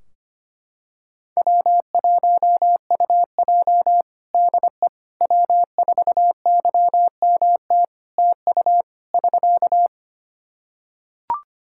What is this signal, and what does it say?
Morse code, heard as a keyed tone, about 25 wpm, 700 Hz. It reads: W1UJ DE W4YMT TU <SK>
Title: Practice QSO - Exchange 4 - Ending - 25wpm